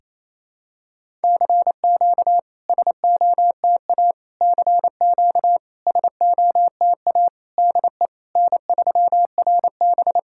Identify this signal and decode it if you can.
Morse code, heard as a keyed tone, about 28 wpm, 700 Hz. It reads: CQ SOTA CQ SOTA DE N3RB
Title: SOTA QSO Practice 28wpm